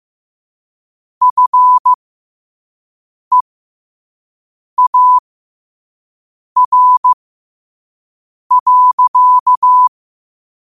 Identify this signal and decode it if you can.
Morse code, heard as a keyed tone, about 15 wpm, spaced as for 5 wpm, 1000 Hz. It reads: FEAR.
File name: qotdfiles / 5wpm